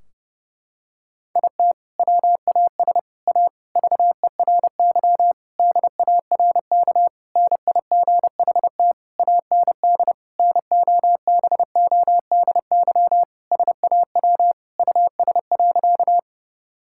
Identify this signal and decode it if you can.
Morse code, heard as a keyed tone, about 32 wpm, 700 Hz. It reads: IT WAS A VERY DARK NIGHT AND NOBODY SAW US.
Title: Sentences from Top 1600 Words 30wpm